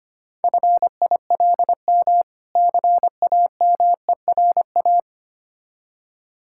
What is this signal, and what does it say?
Morse code, heard as a keyed tone, about 25 wpm, 700 Hz. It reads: FILM CAMERA
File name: practice / 25wpm / Difficult Words - Speed-Racing with Hints - 25wpm